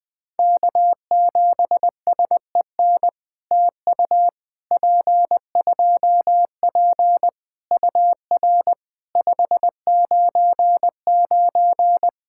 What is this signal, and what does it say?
Morse code, heard as a keyed tone, about 20 wpm, 700 Hz. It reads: K7SEN TU P2P UR 599
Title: POTA QSO Practice - 20wpm